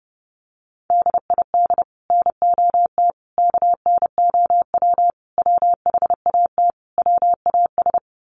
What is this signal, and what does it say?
Morse code, heard as a keyed tone, about 30 wpm, 700 Hz. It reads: DID NOT KNOW WHAT WAS
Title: Sets of 5 Words Encore2 30wpm